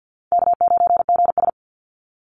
Morse code, heard as a keyed tone, about 48 wpm, 700 Hz. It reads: K9GS
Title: Call Signs Repeated 3x - 50wpm